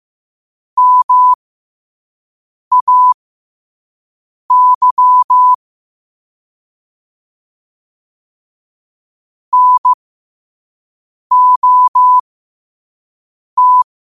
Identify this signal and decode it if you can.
Morse code, heard as a keyed tone, about 15 wpm, spaced as for 5 wpm, 1000 Hz. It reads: MAY NOT
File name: qotdfiles / 5wpm